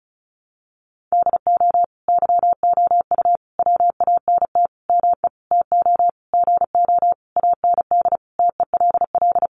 Morse code, heard as a keyed tone, about 34 wpm, 700 Hz. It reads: DO YOU WANT ME TO GO AND TELL
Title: Sentences from Top 200 Words 35wpm